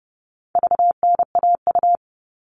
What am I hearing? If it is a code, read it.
Morse code, heard as a keyed tone, about 30 wpm, 700 Hz. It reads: VNAU
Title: All Cut Numbers - 4-Digit 30wpm